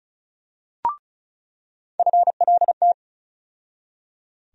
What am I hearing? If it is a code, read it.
Morse code, heard as a keyed tone, about 36 wpm, 700 Hz. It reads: FLT
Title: USA Call Sign Postfixes - 3 Letters 35wpm